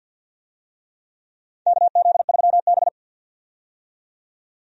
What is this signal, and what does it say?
Morse code, heard as a keyed tone, about 52 wpm, 700 Hz. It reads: KZ3B
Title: USA Call Signs - 4 Characters - 50wpm